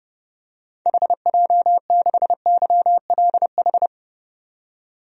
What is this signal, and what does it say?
Morse code, heard as a keyed tone, about 30 wpm, 700 Hz. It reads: HJ6YLH
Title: DXCC-100 Call Signs - 6 Characters - 30wpm